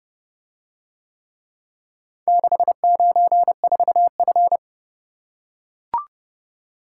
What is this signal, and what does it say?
Morse code, heard as a keyed tone, about 30 wpm, 700 Hz. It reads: 694F